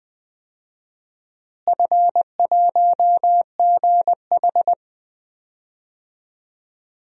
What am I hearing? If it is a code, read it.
Morse code, heard as a keyed tone, about 20 wpm, 700 Hz. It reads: F1GH
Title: DXCC-100 Call Signs - 4 Characters - Repeated 2x - 20wpm